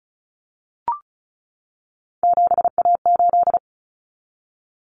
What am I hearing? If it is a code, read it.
Morse code, heard as a keyed tone, about 34 wpm, 700 Hz. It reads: 7A8
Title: Common Cut Numbers - 3-Digit 35wpm